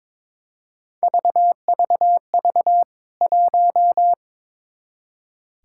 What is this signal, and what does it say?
Morse code, heard as a keyed tone, about 22 wpm, 700 Hz. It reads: VVV 1
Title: K1USN SST QSO Practice 22wpm